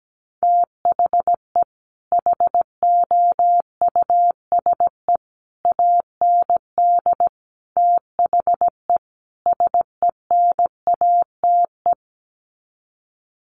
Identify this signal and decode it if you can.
Morse code, heard as a keyed tone, about 17 wpm, 700 Hz. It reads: THE HOUSE AND THE SENATE
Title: Sets of 5 Words Encore2 17wpm